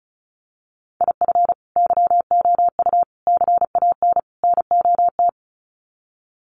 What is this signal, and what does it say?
Morse code, heard as a keyed tone, about 34 wpm, 700 Hz. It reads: IF YOU CAN NOT
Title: Sets of 4 Words 35wpm